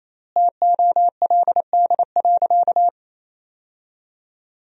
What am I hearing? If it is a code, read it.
Morse code, heard as a keyed tone, about 28 wpm, 700 Hz. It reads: TOLD.
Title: 4x Word-Spacing - Sentences from Top 400 Words - 28wpm